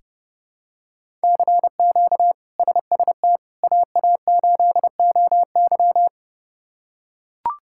Morse code, heard as a keyed tone, about 32 wpm, 700 Hz. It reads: CQ SST AA8OY
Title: K1USN SST QSO Practice - 30wpm